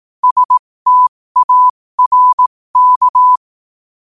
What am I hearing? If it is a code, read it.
Morse code, heard as a keyed tone, about 18 wpm, spaced as for 15 wpm, 1000 Hz. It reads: STARK